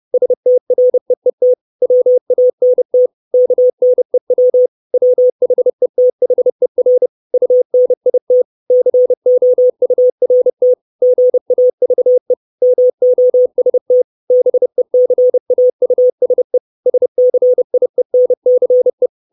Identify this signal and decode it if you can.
Morse code, heard as a keyed tone, about 30 wpm, 500 Hz. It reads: STREET WANT KNEW WHETHER UNIT COURT GAVE MOST BECAUSE SCIENCE
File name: 632 Words 30WPM